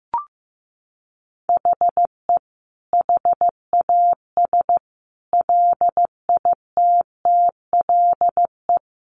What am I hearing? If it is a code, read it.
Morse code, heard as a keyed tone, about 15 wpm, 700 Hz. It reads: HE HAS LITTLE
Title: Sentences from Top 1200 Words 15wpm